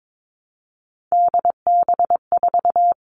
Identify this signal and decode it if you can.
Morse code, heard as a keyed tone, about 22 wpm, 700 Hz. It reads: DB4
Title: Three Challenging Letter-Number Combinations 22wpm